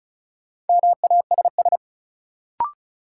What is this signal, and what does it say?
Morse code, heard as a keyed tone, about 36 wpm, 700 Hz. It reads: MASS
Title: Sets of 5 Words Encore - New Word Drill - 35wpm